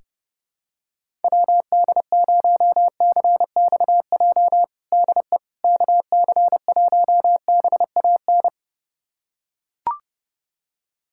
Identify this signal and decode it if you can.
Morse code, heard as a keyed tone, about 30 wpm, 700 Hz. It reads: WD0CXJ DE KC1BAN
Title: USA Callsign DE Callsign - 6 Characters - 30wpm